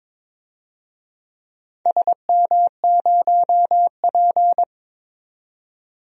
Morse code, heard as a keyed tone, about 22 wpm, 700 Hz. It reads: SM0P
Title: DXCC-100 Call Signs - 4 Characters - 22wpm